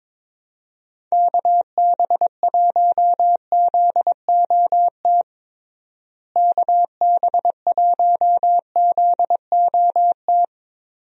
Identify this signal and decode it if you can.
Morse code, heard as a keyed tone, about 22 wpm, 700 Hz. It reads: KB1ZOT KB1ZOT
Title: USA Call Signs - 6 Characters - Repeated 3x - 22wpm